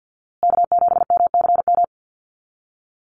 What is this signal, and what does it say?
Morse code, heard as a keyed tone, about 50 wpm, 700 Hz. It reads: K7MCM
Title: USA Call Signs Repeated 3x 50wpm